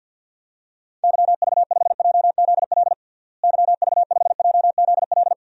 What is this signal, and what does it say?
Morse code, heard as a keyed tone, about 48 wpm, 700 Hz. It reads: YV5JZL YV5JZL
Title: DXCC-100 Call Signs - 6 Characters - Repeated 2x - 50wpm